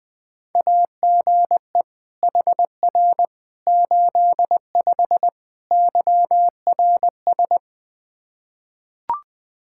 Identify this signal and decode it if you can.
Morse code, heard as a keyed tone, about 20 wpm, 700 Hz. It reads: AGE HR 85 YRS
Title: Practice QSO - Exchange 3 - More Info - 20wpm